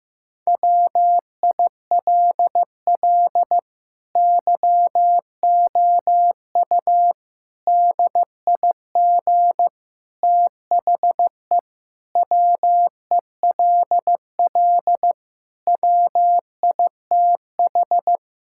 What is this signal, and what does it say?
Morse code, heard as a keyed tone, about 16 wpm, 700 Hz. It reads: WILL YOU DIG THE WELL WITH
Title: Sentences from Top 1600 Words 15wpm